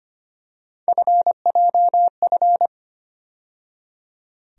Morse code, heard as a keyed tone, about 25 wpm, 700 Hz. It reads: FJF
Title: USA Call Sign Postfixes - 3 Letters 25wpm